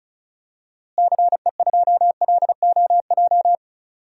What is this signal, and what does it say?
Morse code, heard as a keyed tone, about 36 wpm, 700 Hz. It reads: CE2LOJ
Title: DXCC-100 Call Signs - 6 Characters - Repeated 3x - 35wpm